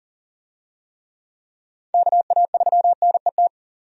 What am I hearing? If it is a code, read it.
Morse code, heard as a keyed tone, about 40 wpm, 700 Hz. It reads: KA3NET
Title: USA Call Signs - 6 Characters - 40wpm